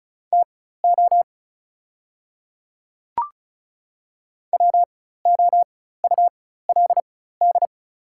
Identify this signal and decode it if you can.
Morse code, heard as a keyed tone, about 36 wpm, spaced as for 16 wpm, 690 Hz. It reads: TO WOULD
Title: 4x Character Spacing - Top 100 Words - 35wpm